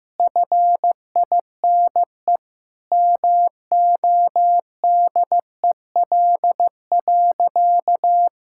Morse code, heard as a keyed tone, about 15 wpm, 700 Hz. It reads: FINE MODEL.
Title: Sentences from Top 1000 Words - Review Part 5 - 15wpm